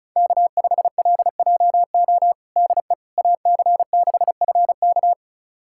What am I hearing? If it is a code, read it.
Morse code, heard as a keyed tone, about 34 wpm, 700 Hz. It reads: K5LJO DE AC6FK
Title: USA Callsign DE Callsign - 5 Characters - 35wpm